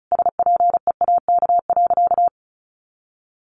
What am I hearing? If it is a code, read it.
Morse code, heard as a keyed tone, about 34 wpm, 700 Hz. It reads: SPEAK.
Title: Sentences from Top 1100 Words 35wpm